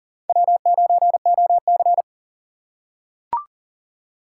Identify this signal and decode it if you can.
Morse code, heard as a keyed tone, about 38 wpm, 700 Hz. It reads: W9OC
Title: USA Call Signs - 2x Character Spacing - 40wpm